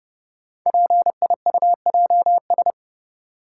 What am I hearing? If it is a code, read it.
Morse code, heard as a keyed tone, about 30 wpm, 700 Hz. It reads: PIUJS
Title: Code Groups - 5 Letters - 30wpm